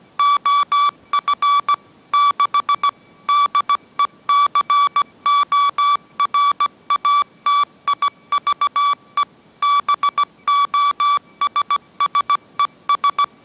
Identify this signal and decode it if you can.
Morse code, heard as a keyed tone, about 17 wpm, 1150 Hz. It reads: OF 6 DECORATIVE BOSSES
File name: mrx / mrx080 18wpm